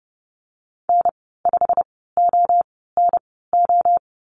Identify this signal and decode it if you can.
Morse code, heard as a keyed tone, about 30 wpm, spaced as for 17 wpm, 700 Hz. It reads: N5ONO
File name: Lesson 15g - Number 5 - Call Signs 30wpm